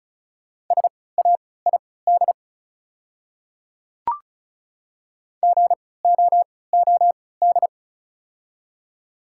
Morse code, heard as a keyed tone, about 34 wpm, spaced as for 20 wpm, 690 Hz. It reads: SAID GOOD
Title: 3x Character Spacing - Top 200 Words - 35wpm